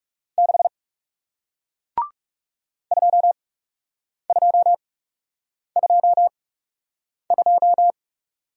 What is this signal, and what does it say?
Morse code, heard as a keyed tone, about 37 wpm, spaced as for 8 wpm, 700 Hz. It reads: 6 2222
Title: Speed-Racing Single Letter-Number 30wpm